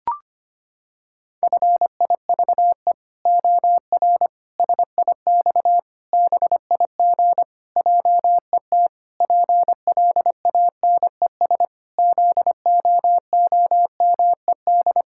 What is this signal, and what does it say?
Morse code, heard as a keyed tone, about 25 wpm, 700 Hz. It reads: FIVE OR SIX BIG JET PLANES ZOOMED
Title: Pangram Challenge v2 - 25wpm